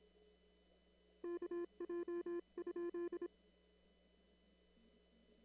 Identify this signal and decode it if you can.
Morse code, heard as a keyed tone, about 26 wpm, 340 Hz. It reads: KJ?
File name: CW / kt5j wpx cw-op-k5pi